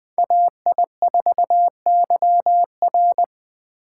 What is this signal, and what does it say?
Morse code, heard as a keyed tone, about 20 wpm, 700 Hz. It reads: AI4YR
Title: Speed Racing - USA Call Signs 17wpm